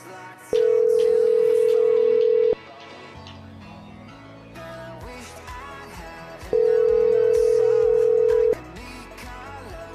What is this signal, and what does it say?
Ringback tone (North America), 440 + 480 Hz.